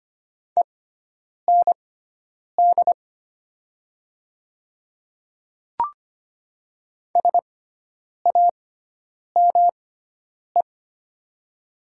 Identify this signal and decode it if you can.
Morse code, heard as a keyed tone, about 25 wpm, spaced as for 9 wpm, 700 Hz. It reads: END SAME
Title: 6x Character Spacing - Top 500 Words - 25wpm